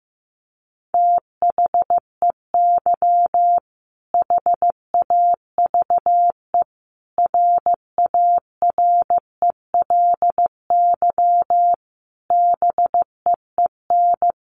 Morse code, heard as a keyed tone, about 15 wpm, 700 Hz. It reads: THEY HAVE RARELY BEEN
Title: Sentences from Top 1400 Words 15wpm